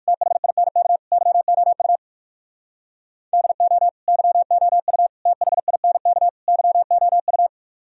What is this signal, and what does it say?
Morse code, heard as a keyed tone, about 59 wpm, 700 Hz. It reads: THINK YOU DO YOU THINK YOU